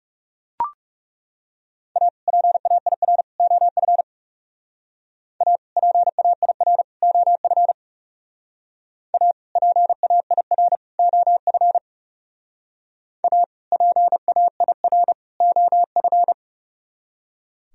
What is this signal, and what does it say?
Morse code, heard as a keyed tone, about 38 wpm, 700 Hz. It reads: A PAIR OF A PAIR OF A PAIR OF A PAIR OF
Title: Speed-Racing - Sets of 3 Words 30wpm